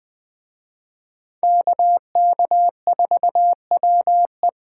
Morse code, heard as a keyed tone, about 20 wpm, 700 Hz. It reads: KK4WE
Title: Speed Racing - USA 2x2 Call Signs 17wpm